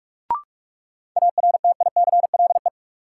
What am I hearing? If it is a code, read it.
Morse code, heard as a keyed tone, about 46 wpm, 700 Hz. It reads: ARTICLE
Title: Sentences from Top 1000 Words - Review Part 5 - New Word Drill - 45wpm